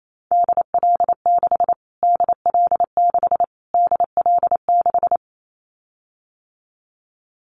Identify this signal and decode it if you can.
Morse code, heard as a keyed tone, about 28 wpm, 700 Hz. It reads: DL6 DL6 DL6
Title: Callsign Prefixes for Top 25 DXCC - Repeated 3x - 28wpm